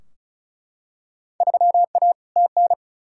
Morse code, heard as a keyed tone, about 34 wpm, 700 Hz. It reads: 3A TN
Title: Field Day Exchanges - 35wpm